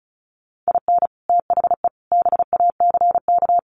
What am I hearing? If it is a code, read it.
Morse code, heard as a keyed tone, about 34 wpm, 700 Hz. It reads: IN THE BACK
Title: Sets of 5 Words Encore2 35wpm